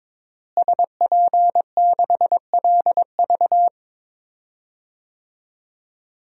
Morse code, heard as a keyed tone, about 22 wpm, 700 Hz. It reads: SP6LV
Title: DXCC-100 Call Signs - 5 Characters - Repeated 2x - 22wpm